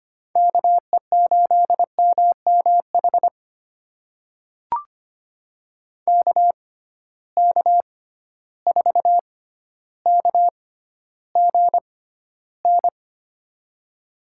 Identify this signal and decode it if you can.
Morse code, heard as a keyed tone, about 25 wpm, spaced as for 13 wpm, 700 Hz. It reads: KE8MMH KK4KGN
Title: USA Call Signs - 6 Characters - 6x Character Spacing - 25wpm